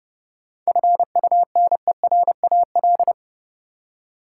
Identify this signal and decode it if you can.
Morse code, heard as a keyed tone, about 30 wpm, 700 Hz. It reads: FUNERAL
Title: Seven Letter Words - 30wpm